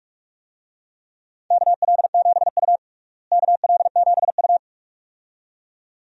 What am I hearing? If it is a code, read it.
Morse code, heard as a keyed tone, about 41 wpm, 700 Hz. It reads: KL7U KL7U
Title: DXCC-100 Call Signs - 4 Characters - Repeated 2x - 45wpm